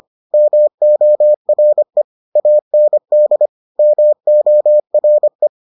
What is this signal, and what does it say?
Morse code, heard as a keyed tone, about 25 wpm, 600 Hz. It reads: MORE AND MORE